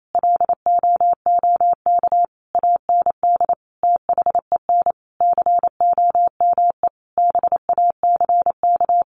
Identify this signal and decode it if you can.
Morse code, heard as a keyed tone, about 28 wpm, 700 Hz. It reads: LOOK AND THEN COME BACK
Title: Sentences from Top 600 Words 28wpm